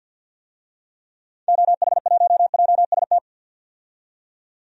Morse code, heard as a keyed tone, about 51 wpm, 700 Hz. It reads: OH1JST
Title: DXCC-100 Call Signs - 6 Characters - 50wpm